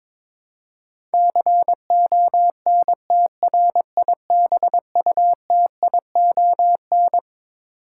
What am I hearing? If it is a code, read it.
Morse code, heard as a keyed tone, about 21 wpm, 700 Hz. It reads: CONTRIBUTION
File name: Twelve Letter Words - 22wpm